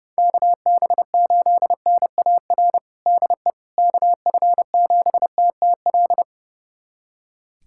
Morse code, heard as a keyed tone, about 30 wpm, 700 Hz. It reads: KB8NAR DE KF7TTL
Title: USA Callsign DE Callsign - 6 Characters - 30wpm